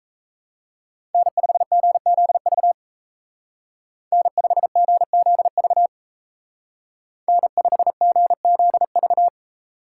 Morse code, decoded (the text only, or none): N5GZV N5GZV N5GZV